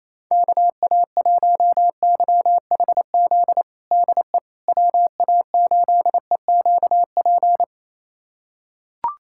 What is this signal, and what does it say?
Morse code, heard as a keyed tone, about 28 wpm, 700 Hz. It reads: KA1YHZ DE WA8EQP
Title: USA Callsign DE Callsign - 6 Characters - 28wpm